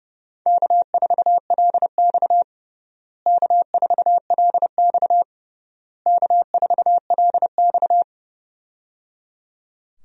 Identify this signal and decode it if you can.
Morse code, heard as a keyed tone, about 30 wpm, 700 Hz. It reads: K4LX K4LX K4LX
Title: USA Call Signs - 4 Characters - Repeated 3x - 30wpm